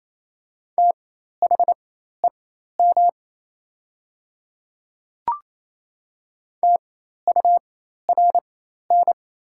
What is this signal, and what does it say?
Morse code, heard as a keyed tone, about 28 wpm, spaced as for 13 wpm, 690 Hz. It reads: THEM TURN
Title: 4x Character Spacing - Top 300 Words - 28wpm